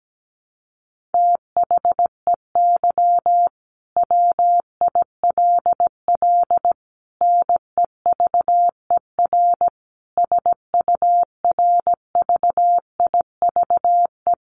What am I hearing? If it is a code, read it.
Morse code, heard as a keyed tone, about 17 wpm, 700 Hz. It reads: THEY WILL NEVER SURVIVE
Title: Sentences from Top 1300 Words 17wpm